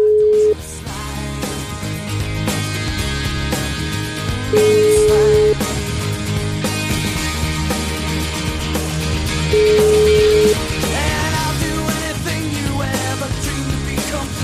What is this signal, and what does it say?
Ringback tone (Europe, 425 Hz), with noise.